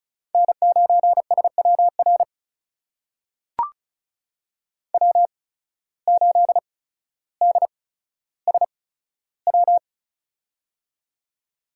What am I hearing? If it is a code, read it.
Morse code, heard as a keyed tone, about 34 wpm, spaced as for 15 wpm, 700 Hz. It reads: N9SWR W8DSW
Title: USA Call Signs - 5 Characters - 8x Character Spacing - 35wpm